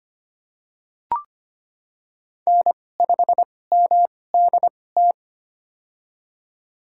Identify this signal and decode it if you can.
Morse code, heard as a keyed tone, about 25 wpm, spaced as for 18 wpm, 700 Hz. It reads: N5MDT
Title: USA Call Signs - 2x Character Spacing - 25wpm